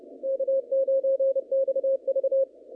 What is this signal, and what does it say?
Morse code, heard as a keyed tone, about 29 wpm, 540 Hz. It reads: K9XV
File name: contests / naqp-cw-k5tr-op-k5pi- -0300